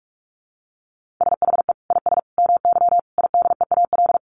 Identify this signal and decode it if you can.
Morse code, heard as a keyed tone, about 44 wpm, 700 Hz. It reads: SHE IS MY IDEAL
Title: Sentences from Top 1400 Words 45wpm